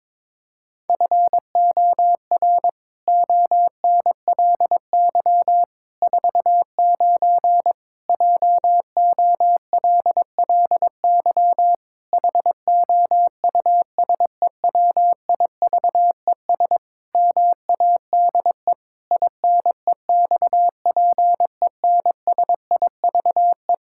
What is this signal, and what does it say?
Morse code, heard as a keyed tone, about 22 wpm, 700 Hz. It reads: FOR ONLY 49 JOLLY HOUSEWIVES MADE INEXPENSIVE
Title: Pangram Challenge v2 - 22wpm